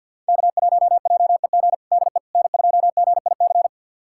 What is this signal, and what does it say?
Morse code, heard as a keyed tone, about 52 wpm, 700 Hz. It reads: K1JEG DE N2GIC
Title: USA Callsign DE Callsign 50wpm